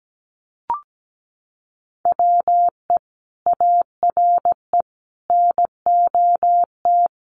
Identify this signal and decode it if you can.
Morse code, heard as a keyed tone, about 17 wpm, 700 Hz. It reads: WE ARE NOT
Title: Sentences from Top 1200 Words 17wpm